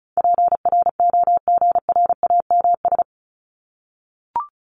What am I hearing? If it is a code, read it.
Morse code, heard as a keyed tone, about 34 wpm, 700 Hz. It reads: PROGRAMS
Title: ES:S Suffix - Top 100 Words with Base Words 35wpm